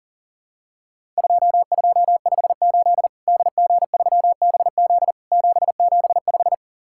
Morse code, heard as a keyed tone, about 37 wpm, 700 Hz. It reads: 2258 DG3BZ 775
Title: Worked All Europe DX Contest - QTC Exchange - 40wpm